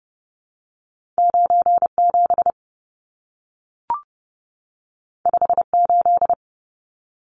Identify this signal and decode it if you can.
Morse code, heard as a keyed tone, about 30 wpm, 700 Hz. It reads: 97 58
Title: Numbers - 2-Digit 30wpm